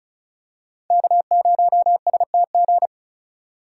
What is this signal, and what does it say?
Morse code, heard as a keyed tone, about 36 wpm, 700 Hz. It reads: K0STG
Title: USA Call Signs - 5 Characters - 2x Character Spacing - 35wpm